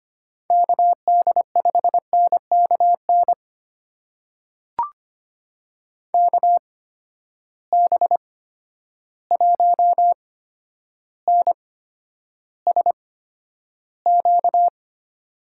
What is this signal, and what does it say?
Morse code, heard as a keyed tone, about 25 wpm, spaced as for 11 wpm, 700 Hz. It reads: KD5NKN KB1NSQ
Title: USA Call Signs - 6 Characters - 8x Character Spacing - 25wpm